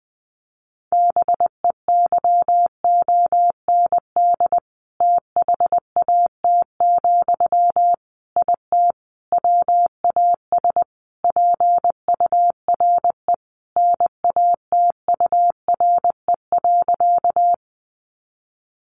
Morse code, heard as a keyed tone, about 20 wpm, 700 Hz. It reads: BEYOND THAT, IT WAS PURE NATURE.
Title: Sentences from Top 1500 Words 20wpm